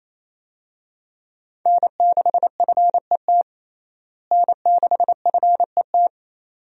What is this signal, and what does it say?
Morse code, heard as a keyed tone, about 28 wpm, 700 Hz. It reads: N6FET N6FET
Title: USA Call Signs - 5 Characters - Repeated 2x - 28wpm